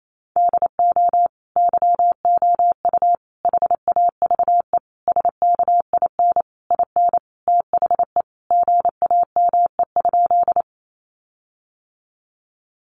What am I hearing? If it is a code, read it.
Morse code, heard as a keyed tone, about 28 wpm, 700 Hz. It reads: DO YOU HAVE SKIN IN THE GAME?
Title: Sentences from Top 1300 Words 28wpm